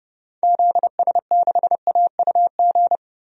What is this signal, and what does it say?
Morse code, heard as a keyed tone, about 30 wpm, 700 Hz. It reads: ZS6AUG
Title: DXCC-100 Call Signs - 6 Characters - Repeated 2x - 30wpm